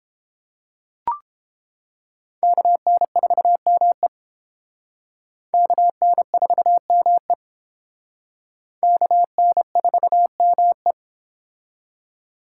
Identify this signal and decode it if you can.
Morse code, heard as a keyed tone, about 29 wpm, 700 Hz. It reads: KN4ME KN4ME KN4ME